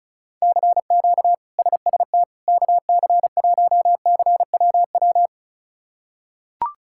Morse code, heard as a keyed tone, about 36 wpm, 700 Hz. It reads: CQ SST KC1CWW
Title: K1USN SST QSO Practice - 35wpm